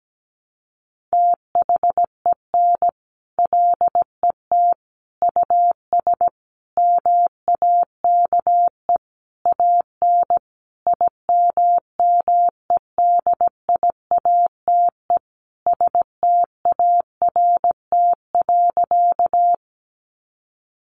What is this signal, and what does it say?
Morse code, heard as a keyed tone, about 17 wpm, 700 Hz. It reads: THEN LET US MAKE AN IMMEDIATE START.
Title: Sentences from Top 1200 Words 17wpm